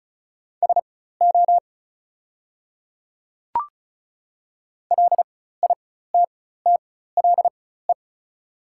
Morse code, heard as a keyed tone, about 36 wpm, spaced as for 16 wpm, 700 Hz. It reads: SO LITTLE
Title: 4x Character Spacing - Top 200 Words - 35wpm